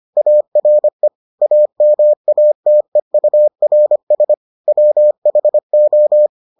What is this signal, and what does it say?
Morse code, heard as a keyed tone, about 25 wpm, 600 Hz. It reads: ARE AMATEURS WHO